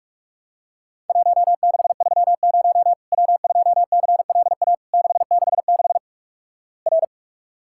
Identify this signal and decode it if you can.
Morse code, heard as a keyed tone, about 46 wpm, 700 Hz. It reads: 1630 W2CLA 666 R
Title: Worked All Europe DX Contest - QTC Exchange - 45wpm